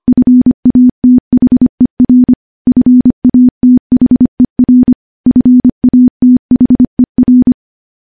Morse code, heard as a keyed tone, about 25 wpm, 250 Hz. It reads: FATHER FATHER FATHER
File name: morse / morse-training-25WPM